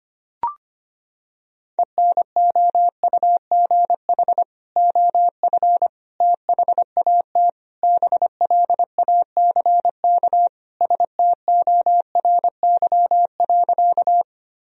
Morse code, heard as a keyed tone, about 25 wpm, 700 Hz. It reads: ENOUGH OF THAT BLACK STORY.
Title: Sentences from Top 1000 Words - Review Part 2 - 25wpm